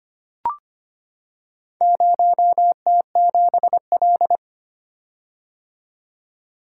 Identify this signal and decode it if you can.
Morse code, heard as a keyed tone, about 25 wpm, 700 Hz. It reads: 0T7L
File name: practice / 25wpm / Four Letter-Number Combinations - 25wpm